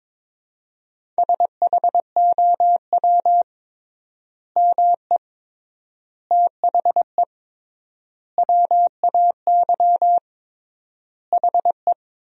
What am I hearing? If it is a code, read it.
Morse code, heard as a keyed tone, about 22 wpm, 700 Hz. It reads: SHOW ME THE WAY HE
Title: 3x Word-Spacing - Sentences from Top 200 Words - 22wpm